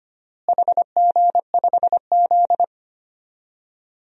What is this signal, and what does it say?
Morse code, heard as a keyed tone, about 25 wpm, 700 Hz. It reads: HG5Z
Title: DXCC-100 Call Signs - 4 Characters - Repeated 2x - 25wpm